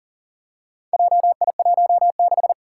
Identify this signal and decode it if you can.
Morse code, heard as a keyed tone, about 40 wpm, 700 Hz. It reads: JI16
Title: Four Challenging Letter-Number Combinations 40wpm